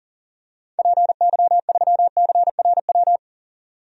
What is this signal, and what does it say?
Morse code, heard as a keyed tone, about 40 wpm, 700 Hz. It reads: PY3CRW
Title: DXCC-100 Call Signs - 6 Characters - 40wpm